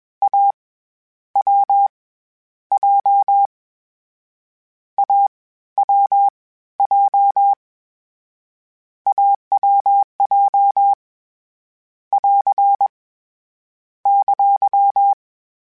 Morse code, heard as a keyed tone, about 21 wpm, 800 Hz. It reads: A W J A W J AWJ <AR> !
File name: Difficult Combos / Difficult Combos-mp3s / AWJ ShortRun@20